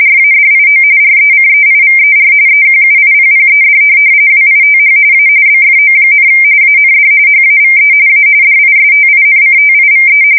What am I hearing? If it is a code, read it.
RTTY, about 45.45 baud. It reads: RADIO RECEIVER EQUIPPED WITH A BFO, AND HAVE A DISTINCTIVE
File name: fsk170 / teletype-RTTY